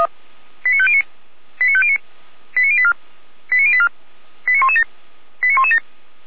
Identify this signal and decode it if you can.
Five-tone selective call (ZVEI), heard as quick runs of five tones, tones of about 70 ms. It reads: 89509 89509 89084 89084 89117 89117